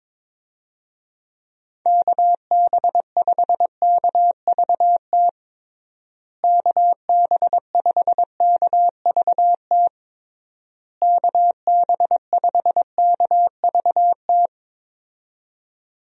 Morse code, heard as a keyed tone, about 22 wpm, 700 Hz. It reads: KB5KVT KB5KVT KB5KVT
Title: USA Call Signs - 6 Characters - Repeated 3x - 22wpm